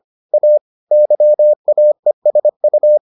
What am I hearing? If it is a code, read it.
Morse code, heard as a keyed tone, about 25 wpm, 600 Hz. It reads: A YAESU